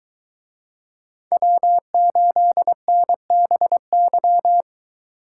Morse code, heard as a keyed tone, about 22 wpm, 700 Hz. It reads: W8NBY